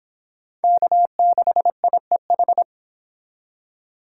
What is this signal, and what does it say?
Morse code, heard as a keyed tone, about 28 wpm, 700 Hz. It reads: K6IEH